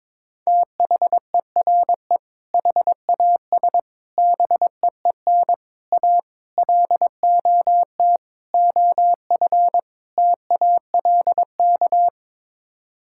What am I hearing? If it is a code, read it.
Morse code, heard as a keyed tone, about 22 wpm, 700 Hz. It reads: THERE HAS BEEN A LOT OF TALK
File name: practice / 22wpm / Sets of 7 Words - 22wpm